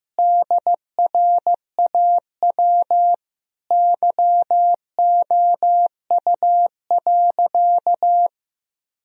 Morse code, heard as a keyed tone, about 15 wpm, 700 Hz. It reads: DRAW YOU.
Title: Sentences from Top 1000 Words - Review Part 1 - 15wpm